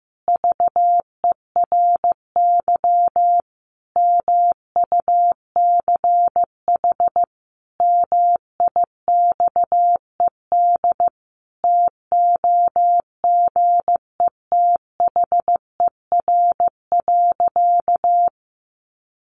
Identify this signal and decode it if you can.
Morse code, heard as a keyed tone, about 15 wpm, 700 Hz. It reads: VERY MUCH MIXED TOGETHER.
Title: Sentences from Top 1500 Words 15wpm